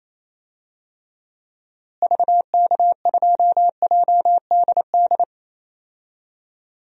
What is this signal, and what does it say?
Morse code, heard as a keyed tone, about 28 wpm, 700 Hz. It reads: VK2JDD
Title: DXCC-100 Call Signs - 6 Characters - Repeated 2x - 28wpm